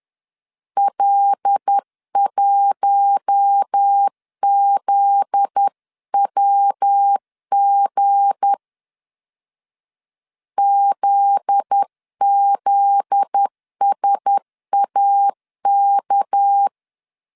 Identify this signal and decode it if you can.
Morse code, heard as a keyed tone, about 10 wpm, 800 Hz. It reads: L1ZWG ZZSAK
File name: LEZIONI / 7 GF9Z / 7 GF9Z 10wpm 2611